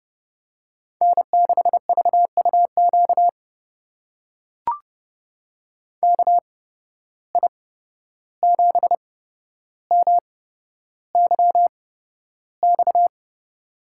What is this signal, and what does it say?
Morse code, heard as a keyed tone, about 30 wpm, spaced as for 12 wpm, 700 Hz. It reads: N6VUQ KI7MYX
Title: USA Call Signs - 8x Character Spacing - 30wpm